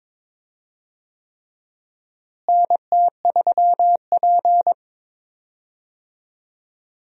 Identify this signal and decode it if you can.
Morse code, heard as a keyed tone, about 22 wpm, 700 Hz. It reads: NT3P
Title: USA Call Signs - 4 Characters - Repeated 3x - 22wpm